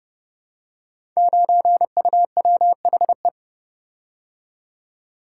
Morse code, heard as a keyed tone, about 30 wpm, 700 Hz. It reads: 9UWHE
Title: Code Groups - 5 Letter:Numbers - 30wpm